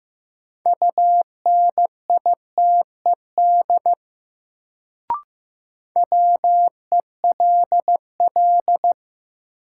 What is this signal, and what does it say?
Morse code, heard as a keyed tone, about 15 wpm, 700 Hz. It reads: UNITED WELL
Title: Sets of 2 Words - New Word Drill - 15wpm